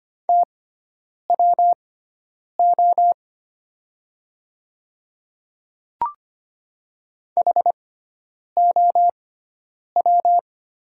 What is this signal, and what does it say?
Morse code, heard as a keyed tone, about 25 wpm, spaced as for 9 wpm, 700 Hz. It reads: TWO HOW
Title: 6x Character Spacing - Top 100 Words - 25wpm